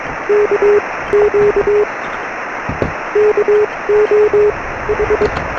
Morse code, heard as a keyed tone, about 21 wpm, spaced as for 17 wpm, 400 Hz, with noise. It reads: KQ KO H
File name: sm7lcb / 0432 ly2va 13cm